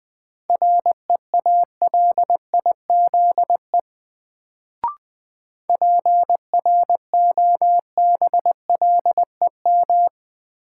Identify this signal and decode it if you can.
Morse code, heard as a keyed tone, about 20 wpm, 700 Hz. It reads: REALIZE PROBLEM
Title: Sentences from Top 1000 Words - New Word Drill - 20wpm